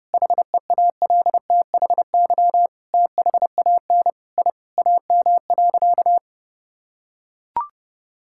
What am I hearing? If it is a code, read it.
Morse code, heard as a keyed tone, about 32 wpm, 700 Hz. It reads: HEALTHY THAN I AM.
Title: Sentences from Top 1000 Words - Review Part 5 - 30wpm